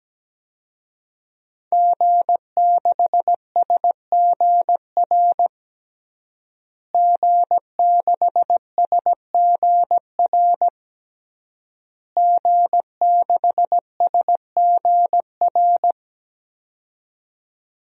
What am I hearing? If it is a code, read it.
Morse code, heard as a keyed tone, about 17 wpm, 700 Hz. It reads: G6SGR G6SGR G6SGR
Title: DXCC-100 Call Signs - 5 Characters - Repeated 3x - 17wpm